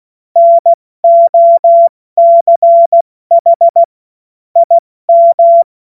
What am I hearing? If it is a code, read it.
Morse code, heard as a keyed tone, about 16 wpm, 680 Hz. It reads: NOCH IM
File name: Morsekurs / Witze / Witz 09. Dezember 2025 70 F80